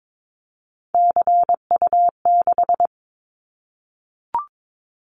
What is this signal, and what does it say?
Morse code, heard as a keyed tone, about 22 wpm, 700 Hz. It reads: CU6
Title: Callsign Prefixes for Top 50 DXCC - 22wpm